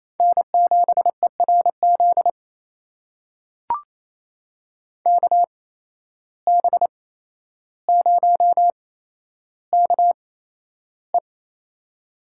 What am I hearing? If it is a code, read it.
Morse code, heard as a keyed tone, about 27 wpm, spaced as for 12 wpm, 700 Hz. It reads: N7ERZ KB0KE
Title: USA Call Signs - 5 Characters - 8x Character Spacing - 28wpm